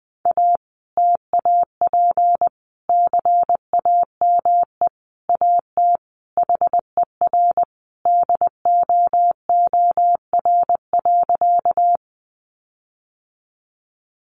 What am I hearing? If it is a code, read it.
Morse code, heard as a keyed tone, about 20 wpm, 700 Hz. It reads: A TAP CAME AT HER DOOR.
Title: Sentences from Top 1400 Words 20wpm